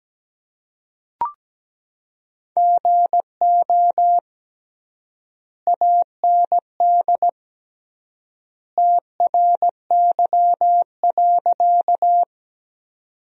Morse code, heard as a keyed tone, about 17 wpm, 700 Hz. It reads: GO AND TRY.
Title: 3x Word-Spacing - Sentences from Top 200 Words - 17wpm